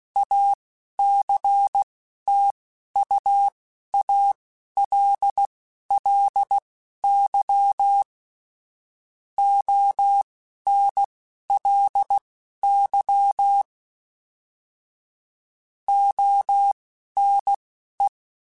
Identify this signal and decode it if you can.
Morse code, heard as a keyed tone, about 16 wpm, spaced as for 11 wpm, 800 Hz. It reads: ACTUALLY ONLY ONE
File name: Technical Subjects / Technical Subjects-mp3s / AND Gate @10